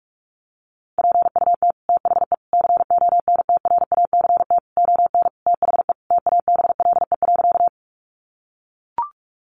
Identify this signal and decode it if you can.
Morse code, heard as a keyed tone, about 44 wpm, 700 Hz. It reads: PUT THE CONTRACT ON THE TABLE.
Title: Sentences from Top 1400 Words 45wpm